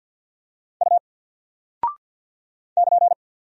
Morse code, heard as a keyed tone, about 50 wpm, 700 Hz. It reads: U <KN>
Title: Single Character (Letter:Number:Punctuation:Prosign) 50wpm